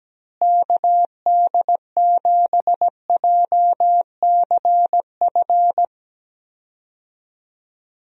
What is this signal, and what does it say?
Morse code, heard as a keyed tone, about 17 wpm, 700 Hz. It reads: KD7JCF
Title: USA Call Signs - 6 Characters - Repeated 2x - 17wpm